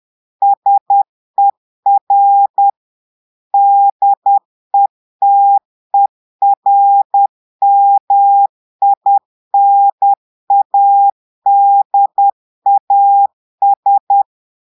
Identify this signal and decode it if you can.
Morse code, heard as a keyed tone, about 10 wpm, 800 Hz. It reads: SER DETERMINADAS
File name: portugais texte en clair 2 vitesse 10 mots minutes